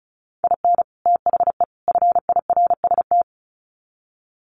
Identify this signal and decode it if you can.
Morse code, heard as a keyed tone, about 34 wpm, 700 Hz. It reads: IN THE FIRST
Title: Sets of 3 Words 35wpm